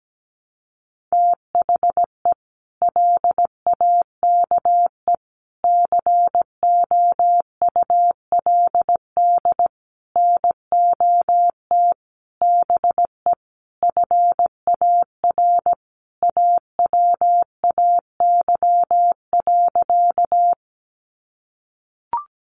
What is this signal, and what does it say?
Morse code, heard as a keyed tone, about 17 wpm, 700 Hz. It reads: THE LAKE COULD NOT BE FAR AWAY.
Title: Sentences from Top 1100 Words 17wpm